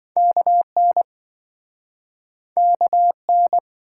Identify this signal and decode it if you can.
Morse code, heard as a keyed tone, about 28 wpm, 700 Hz. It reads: KN KN